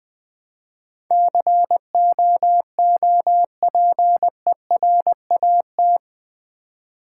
Morse code, heard as a keyed tone, about 20 wpm, 700 Hz. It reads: COOPERAT
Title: Build A Word - 4 Letters At A Time - 11-Letter Words - 20wpm